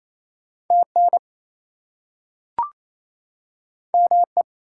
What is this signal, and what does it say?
Morse code, heard as a keyed tone, about 28 wpm, 700 Hz. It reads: TN ME
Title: US States and Canadian Provinces 28wpm